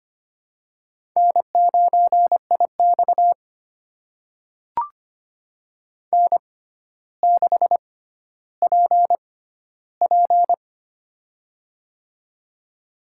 Morse code, heard as a keyed tone, about 25 wpm, spaced as for 13 wpm, 700 Hz. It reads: N9IX N6PP
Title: USA Call Signs - 4 Characters - 6x Character Spacing - 25wpm